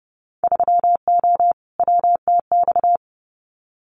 Morse code, heard as a keyed tone, about 30 wpm, 700 Hz. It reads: 3O WTX
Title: Winter Field Day Exchanges Encore 30wpm